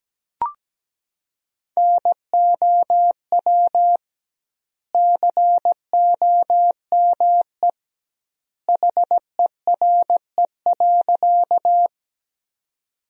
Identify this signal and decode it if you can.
Morse code, heard as a keyed tone, about 17 wpm, 700 Hz. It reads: NOW COME HERE.
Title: 2x Word-Spacing - Sentences from Top 200 Words - 17wpm